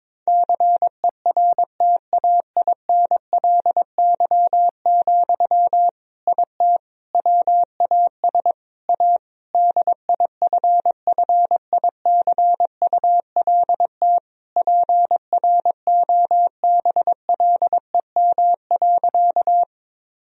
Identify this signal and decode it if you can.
Morse code, heard as a keyed tone, about 22 wpm, 700 Hz. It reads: CERTAINLY, IT WAS A DIFFICULT PROBLEM.
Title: Sentences from Top 1000 Words - Review Part 2 - 22wpm